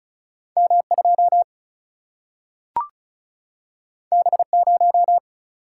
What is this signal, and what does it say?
Morse code, heard as a keyed tone, about 36 wpm, 700 Hz. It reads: M2 B0
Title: Single-Letter Single-Number 35wpm